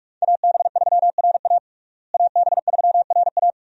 Morse code, heard as a keyed tone, about 47 wpm, 700 Hz. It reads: AB3RA AB3RA
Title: USA Call Signs - 5 Characters - Repeated 3x - 45wpm